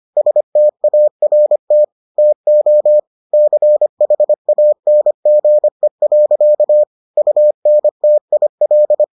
Morse code, heard as a keyed tone, about 25 wpm, 600 Hz. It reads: START TO CHANGE. UNTIL